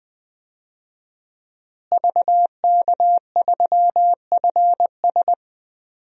Morse code, heard as a keyed tone, about 21 wpm, 700 Hz. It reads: VK3FS
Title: USA Call Signs - 5 Characters - Repeated 2x - 20wpm